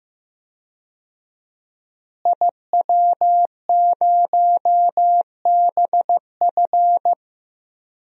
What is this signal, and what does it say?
Morse code, heard as a keyed tone, about 16 wpm, 700 Hz. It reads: IW0BF
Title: DXCC-100 Call Signs - 5 Characters - Repeated 2x - 15wpm